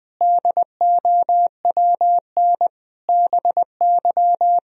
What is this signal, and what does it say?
Morse code, heard as a keyed tone, about 20 wpm, 700 Hz. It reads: DOWN BY
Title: Sentences from Top 1000 Words - Review Part 4 - 20wpm